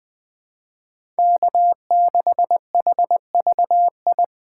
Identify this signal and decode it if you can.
Morse code, heard as a keyed tone, about 20 wpm, 700 Hz. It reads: K6HVI
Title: USA Call Signs - 5 Characters - 2x Character Spacing - 20wpm